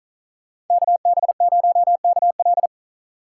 Morse code, heard as a keyed tone, about 44 wpm, 700 Hz. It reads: KB0KL